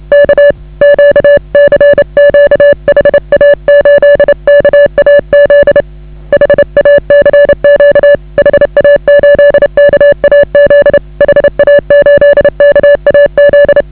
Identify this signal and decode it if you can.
Morse code, heard as a keyed tone, about 30 wpm, 560 Hz. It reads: K Q CQ HA8KAZ HA CQ HA8KAZ HA8KAZ